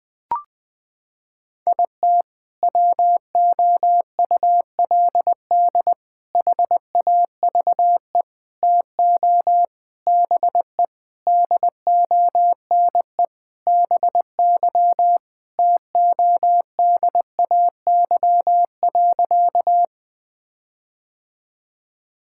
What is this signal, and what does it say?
Morse code, heard as a keyed tone, about 21 wpm, 700 Hz. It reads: IT WOULD HAVE TO BE DONE BY TODAY.
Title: Sentences with 4-Word Phrases - 20wpm